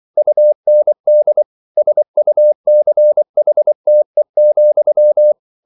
Morse code, heard as a keyed tone, about 24 wpm, 600 Hz. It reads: UND SUCHTE,